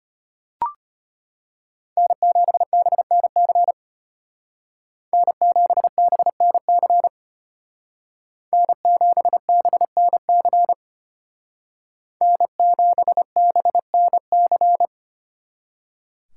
Morse code, heard as a keyed tone, about 33 wpm, 700 Hz. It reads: N7BNC N7BNC N7BNC N7BNC